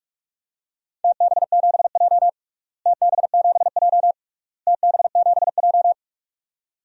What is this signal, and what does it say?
Morse code, heard as a keyed tone, about 42 wpm, 700 Hz. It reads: TB7J TB7J TB7J
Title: DXCC-100 Call Signs - 4 Characters - Repeated 3x - 45wpm